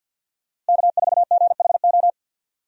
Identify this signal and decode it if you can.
Morse code, heard as a keyed tone, about 47 wpm, 700 Hz. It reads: K4GHO
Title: USA Call Signs - 5 Characters - Repeated 2x - 50wpm